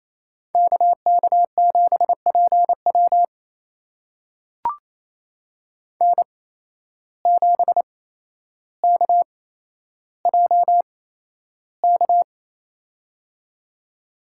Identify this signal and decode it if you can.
Morse code, heard as a keyed tone, about 27 wpm, spaced as for 12 wpm, 700 Hz. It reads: KK7PW N7KJK
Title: USA Call Signs - 5 Characters - 8x Character Spacing - 28wpm